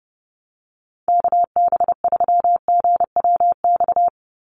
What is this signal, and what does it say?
Morse code, heard as a keyed tone, about 30 wpm, 700 Hz. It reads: KB3GWX
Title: USA Call Signs 30wpm